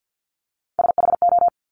Morse code, heard as a keyed tone, about 50 wpm, 700 Hz. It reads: SHO
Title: Top Three Letter Combinations in Words 50wpm